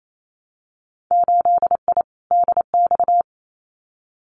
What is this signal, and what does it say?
Morse code, heard as a keyed tone, about 28 wpm, 700 Hz. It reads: 8I DX
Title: Winter Field Day Exchanges Encore 28wpm